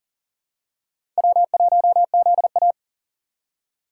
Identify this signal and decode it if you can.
Morse code, heard as a keyed tone, about 39 wpm, 690 Hz. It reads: W1ZA
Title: USA Call Signs - 4x Character Spacing - 40wpm